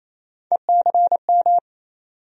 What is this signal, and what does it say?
Morse code, heard as a keyed tone, about 28 wpm, 700 Hz. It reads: ECM
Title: USA Call Sign Postfixes - 3 Letters 28wpm